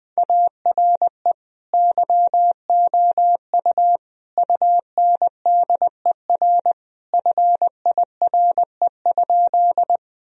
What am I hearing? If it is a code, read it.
Morse code, heard as a keyed tone, about 20 wpm, 700 Hz. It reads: ARE YOU UNDER FIRE?
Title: Sentences from Top 1000 Words - Review Part 1 - 20wpm